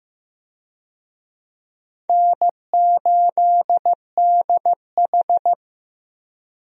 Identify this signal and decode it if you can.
Morse code, heard as a keyed tone, about 15 wpm, 700 Hz. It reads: N8DH
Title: USA Call Signs - 4 Characters - 15wpm